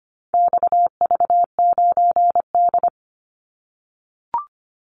Morse code, heard as a keyed tone, about 25 wpm, 700 Hz. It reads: XV9D
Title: Call Signs - 25wpm